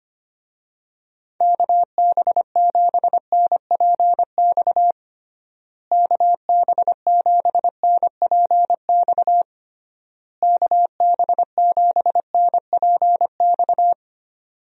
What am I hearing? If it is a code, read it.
Morse code, heard as a keyed tone, about 25 wpm, 700 Hz. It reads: KB7NPX KB7NPX KB7NPX
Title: USA Call Signs - 6 Characters - Repeated 3x - 25wpm